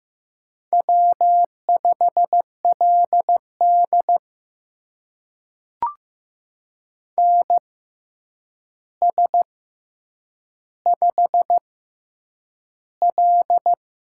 Morse code, heard as a keyed tone, about 15 wpm, spaced as for 8 wpm, 700 Hz. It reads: W5LD NS5L
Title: USA Call Signs - 4 Characters - 6x Character Spacing - 15wpm